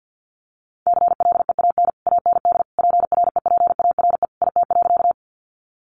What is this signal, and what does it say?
Morse code, heard as a keyed tone, about 50 wpm, 700 Hz. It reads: CLEAN AND PREPARE IT.
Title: Sentences from Top 1100 Words 50wpm